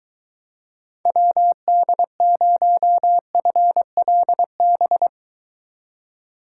Morse code, heard as a keyed tone, about 23 wpm, 700 Hz. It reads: WD0FLB